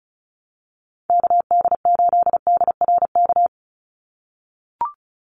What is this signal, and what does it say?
Morse code, heard as a keyed tone, about 34 wpm, 700 Hz. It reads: KD8DRK
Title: USA Call Signs 35wpm